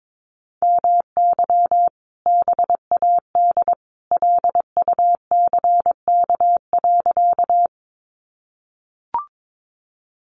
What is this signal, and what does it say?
Morse code, heard as a keyed tone, about 22 wpm, 700 Hz. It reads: MY BAD LUCK.
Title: Sentences from Top 1500 Words 22wpm